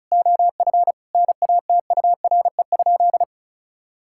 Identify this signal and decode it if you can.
Morse code, heard as a keyed tone, about 34 wpm, 700 Hz. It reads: OF NATURE?
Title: Sentences from Top 1000 Words - Review Part 5 - 35wpm